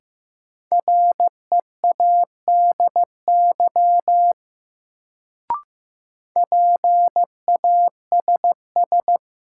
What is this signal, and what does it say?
Morse code, heard as a keyed tone, about 15 wpm, 700 Hz. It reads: READY PASS
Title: Sentences from Top 400 Words - New Word Drill - 15wpm